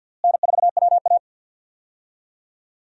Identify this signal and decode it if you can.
Morse code, heard as a keyed tone, about 46 wpm, 700 Hz. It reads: N4WA
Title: USA Call Signs - 4 Characters - Repeated 3x - 50wpm